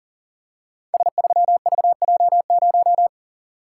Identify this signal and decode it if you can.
Morse code, heard as a keyed tone, about 38 wpm, 700 Hz. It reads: S3VJ0
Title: Code Groups - 5 Letter:Numbers - 40wpm